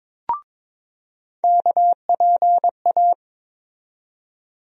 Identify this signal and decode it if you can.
Morse code, heard as a keyed tone, about 22 wpm, 700 Hz. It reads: KPA
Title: USA Call Sign Postfixes - 3 Letters 22wpm